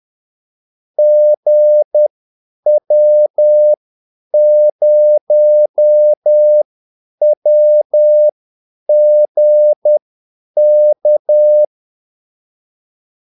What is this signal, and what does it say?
Morse code, heard as a keyed tone, about 10 wpm, spaced as for 8 wpm, 600 Hz. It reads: GW0WGK
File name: Callsigns-10-8wpm